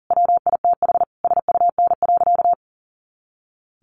Morse code, heard as a keyed tone, about 40 wpm, 700 Hz. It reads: WITH SUN.
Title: A Sound Of Thunder 40wpm